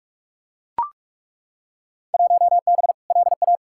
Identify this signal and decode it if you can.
Morse code, heard as a keyed tone, about 41 wpm, 700 Hz. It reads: 1B LA
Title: Field Day Exchanges Encore - 45wpm